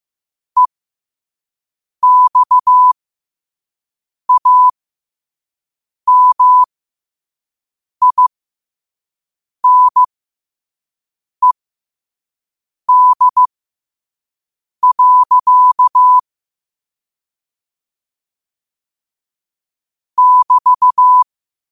Morse code, heard as a keyed tone, about 15 wpm, spaced as for 5 wpm, 1000 Hz. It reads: EXAMINED. <BT>
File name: qotdfiles / 5wpm